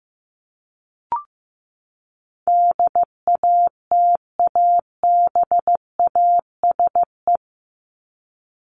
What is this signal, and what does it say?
Morse code, heard as a keyed tone, about 15 wpm, 700 Hz. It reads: DATABASE
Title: Top 1751-2000 Words 15wpm